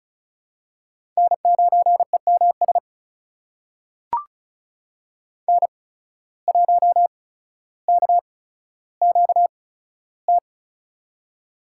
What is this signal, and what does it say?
Morse code, heard as a keyed tone, about 36 wpm, spaced as for 15 wpm, 700 Hz. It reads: N9EMS N1KQT
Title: USA Call Signs - 5 Characters - 8x Character Spacing - 35wpm